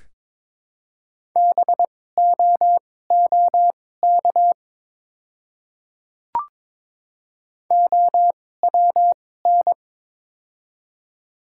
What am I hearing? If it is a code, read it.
Morse code, heard as a keyed tone, about 22 wpm, spaced as for 16 wpm, 700 Hz. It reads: BOOK OWN
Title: 2x Character Spacing - Top 300 Words - 22wpm